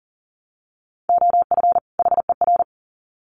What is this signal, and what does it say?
Morse code, heard as a keyed tone, about 40 wpm, 700 Hz. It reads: OF HER
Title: Sets of 2 Words 40wpm